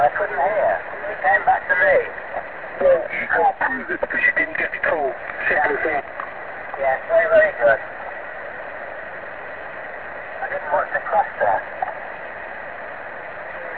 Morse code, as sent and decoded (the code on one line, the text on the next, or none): none